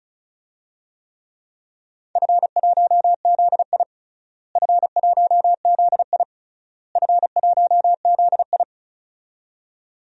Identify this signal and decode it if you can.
Morse code, heard as a keyed tone, about 34 wpm, 700 Hz. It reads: F1ZI F1ZI F1ZI
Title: DXCC-100 Call Signs - 4 Characters - Repeated 3x - 35wpm